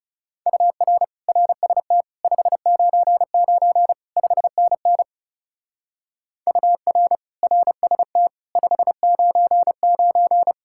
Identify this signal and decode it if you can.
Morse code, heard as a keyed tone, about 36 wpm, 700 Hz. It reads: UR RST 599 5NN UR RST 599
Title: Speed-Racing Common QSO Phrases - 30wpm